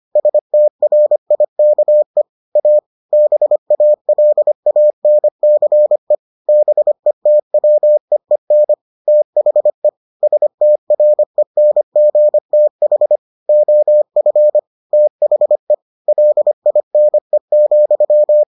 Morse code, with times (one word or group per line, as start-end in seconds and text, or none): 0.2-2.2 STRIKE
2.6-2.8 A
3.1-6.2 BALANCE
6.5-8.7 BETWEEN
9.1-9.9 THE
10.2-13.2 STRENGTH
13.5-14.6 OF
14.9-15.7 THE
16.1-18.4 LINE,